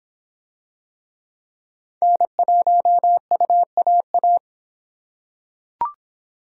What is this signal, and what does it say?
Morse code, heard as a keyed tone, about 26 wpm, 700 Hz. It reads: N1UAA